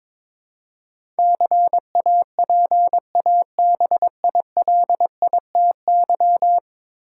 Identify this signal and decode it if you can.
Morse code, heard as a keyed tone, about 22 wpm, 700 Hz. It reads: CAPABILITY
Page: morse-code-ninja-practice-22wpm